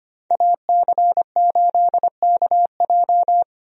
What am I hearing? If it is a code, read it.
Morse code, heard as a keyed tone, about 25 wpm, 700 Hz. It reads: AC8KJ
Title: USA Call Signs - 5 Characters - 25wpm